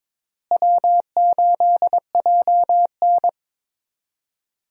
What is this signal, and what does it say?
Morse code, heard as a keyed tone, about 22 wpm, 700 Hz. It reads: W8JN
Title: USA Call Signs - 4 Characters - Repeated 2x - 22wpm